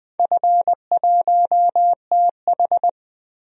Morse code, heard as a keyed tone, about 20 wpm, 700 Hz. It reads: F1TH
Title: DXCC-100 Call Signs - 4 Characters - Repeated 3x - 20wpm